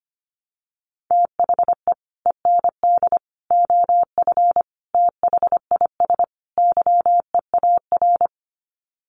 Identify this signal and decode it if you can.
Morse code, heard as a keyed tone, about 25 wpm, 700 Hz. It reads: THE END OF THIS YEAR
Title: Sets of 5 Words Encore 25wpm